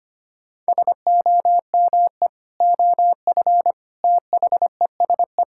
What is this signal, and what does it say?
Morse code, heard as a keyed tone, about 25 wpm, 700 Hz. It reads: SOME OF THESE
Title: Sentences from Top 1000 Words - Review Part 6 - 25wpm